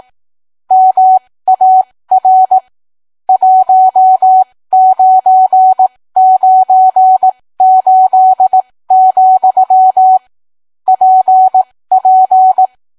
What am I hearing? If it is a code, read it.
Morse code, heard as a keyed tone, about 18 wpm, spaced as for 15 wpm, 750 Hz. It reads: MAR 1998, PP